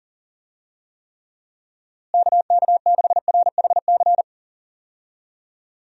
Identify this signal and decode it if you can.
Morse code, heard as a keyed tone, about 39 wpm, 700 Hz. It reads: KK6RHC
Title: USA Call Signs - 6 Characters - Repeated 3x - 40wpm